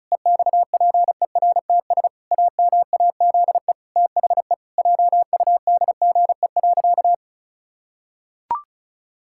Morse code, heard as a keyed tone, about 36 wpm, 700 Hz. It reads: EXPERTS AMAZE THE JUDGE.
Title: Pangram Challenge v2 - 35wpm